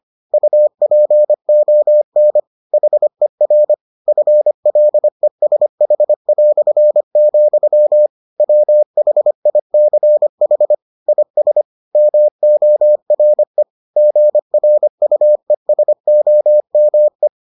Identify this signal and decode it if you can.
Morse code, heard as a keyed tone, about 25 wpm, 600 Hz. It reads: UPON HER FLESH", WHICH IS MORE GRUESOME